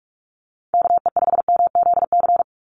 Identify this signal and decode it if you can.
Morse code, heard as a keyed tone, about 44 wpm, 690 Hz. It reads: KE5MZC
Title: USA Call Signs 45wpm